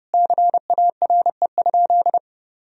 Morse code, heard as a keyed tone, about 30 wpm, 700 Hz. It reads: CARE?
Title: Sentences from Top 1000 Words - Review Part 3 - 30wpm